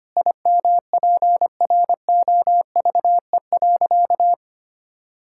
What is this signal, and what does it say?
Morse code, heard as a keyed tone, about 25 wpm, 700 Hz. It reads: IMPROVE.
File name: practice / 25wpm / Sentences with 4-Word Phrases - 25wpm